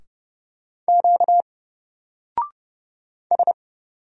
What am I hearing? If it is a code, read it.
Morse code, heard as a keyed tone, about 30 wpm, 700 Hz. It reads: Q S
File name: Single-Letter-Number-30wpm